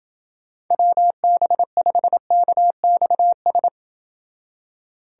Morse code, heard as a keyed tone, about 27 wpm, 700 Hz. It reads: WB5KXS